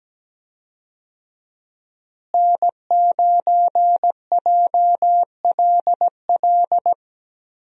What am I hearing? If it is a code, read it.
Morse code, heard as a keyed tone, about 17 wpm, 700 Hz. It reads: N9JLL
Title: USA Call Signs - 5 Characters - Repeated 3x - 17wpm